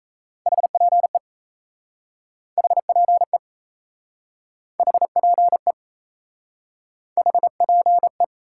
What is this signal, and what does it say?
Morse code, heard as a keyed tone, about 38 wpm, 700 Hz. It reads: HPE HPE HPE HPE